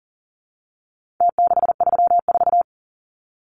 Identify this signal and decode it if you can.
Morse code, heard as a keyed tone, about 40 wpm, 700 Hz. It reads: T634
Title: Numbers - Contest Style - 4-Digit 40wpm